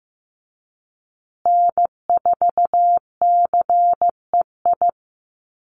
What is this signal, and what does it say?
Morse code, heard as a keyed tone, about 15 wpm, 700 Hz. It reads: N4CEI
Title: USA Call Signs Repeated 2x 15wpm